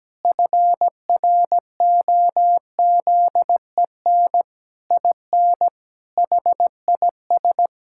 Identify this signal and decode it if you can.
Morse code, heard as a keyed tone, about 17 wpm, 700 Hz. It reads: FROZEN IN HIS
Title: Pangram Challenge v2 - 17wpm